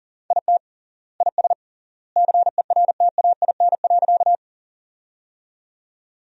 Morse code, heard as a keyed tone, about 40 wpm, 700 Hz. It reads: IT IS CERTAIN.
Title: 3x Word-Spacing - Sentences from Top 500 Words - 40wpm